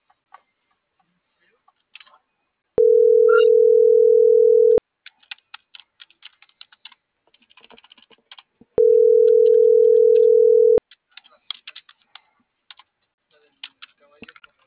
Ringback tone (North America), 440 + 480 Hz.